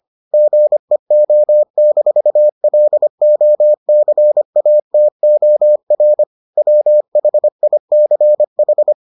Morse code, heard as a keyed tone, about 25 wpm, 600 Hz. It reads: GEO-LOCATOR WHICH